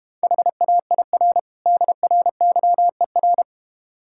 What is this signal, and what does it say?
Morse code, heard as a keyed tone, about 32 wpm, 700 Hz. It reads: HAIR DRYER